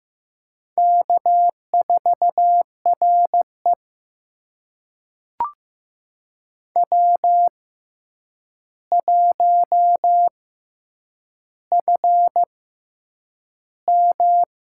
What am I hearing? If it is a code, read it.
Morse code, heard as a keyed tone, about 15 wpm, spaced as for 8 wpm, 700 Hz. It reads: K4RE W1FM
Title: USA Call Signs - 4 Characters - 6x Character Spacing - 15wpm